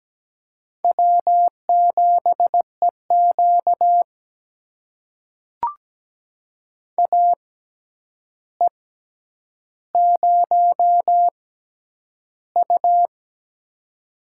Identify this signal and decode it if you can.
Morse code, heard as a keyed tone, about 17 wpm, spaced as for 9 wpm, 700 Hz. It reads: W7EQ AE0U
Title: USA Call Signs - 4 Characters - 6x Character Spacing - 17wpm